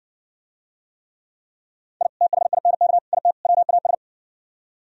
Morse code, heard as a keyed tone, about 60 wpm, 700 Hz. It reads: I THINK IT WAS